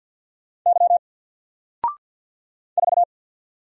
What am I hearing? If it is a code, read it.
Morse code, heard as a keyed tone, about 50 wpm, 700 Hz. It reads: Y 4
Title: Single Letter-Number 50wpm